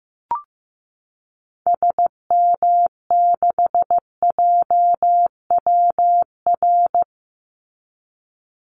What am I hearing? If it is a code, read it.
Morse code, heard as a keyed tone, about 15 wpm, 700 Hz. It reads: SM6JWR
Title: Call Signs Repeated 3x - 15wpm